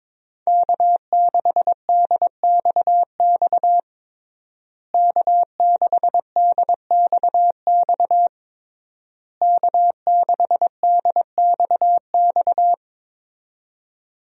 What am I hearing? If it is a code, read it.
Morse code, heard as a keyed tone, about 22 wpm, 700 Hz. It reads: K6DXX K6DXX K6DXX
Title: USA Call Signs - 5 Characters - Repeated 3x - 22wpm